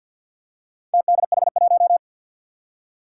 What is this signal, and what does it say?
Morse code, heard as a keyed tone, about 48 wpm, 700 Hz. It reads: TDH1
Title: Four Challenging Letter-Number Combinations 50wpm